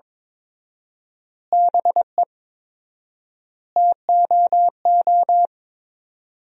Morse code, heard as a keyed tone, about 23 wpm, 700 Hz. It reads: BE TOO
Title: 4x Word-Spacing - Sentences from Top 300 Words - 22wpm